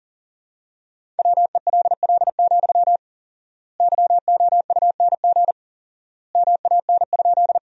Morse code, heard as a keyed tone, about 40 wpm, 700 Hz. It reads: WELL, YOUNG MAN?
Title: 4x Word-Spacing - Sentences from Top 400 Words - 40wpm